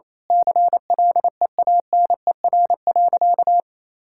Morse code, heard as a keyed tone, about 27 wpm, 700 Hz. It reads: CLEANER.
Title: Pangram Challenge v2 - 28wpm